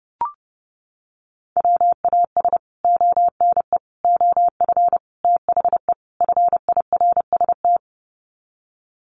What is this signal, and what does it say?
Morse code, heard as a keyed tone, about 30 wpm, 700 Hz. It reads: WAS ONE OF THE FIRST
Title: Sets of 5 Words 30wpm